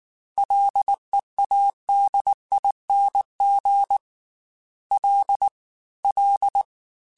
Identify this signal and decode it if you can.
Morse code, heard as a keyed tone, about 19 wpm, 800 Hz. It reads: LEADING L L